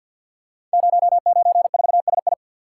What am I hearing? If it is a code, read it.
Morse code, heard as a keyed tone, about 53 wpm, 700 Hz. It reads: 094SI
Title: Code Groups - 5 Letter:Numbers - 50wpm